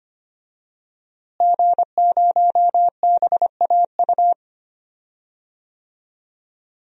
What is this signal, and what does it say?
Morse code, heard as a keyed tone, about 25 wpm, 700 Hz. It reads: G0BAU
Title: DXCC-100 Call Signs - 5 Characters - Repeated 3x - 25wpm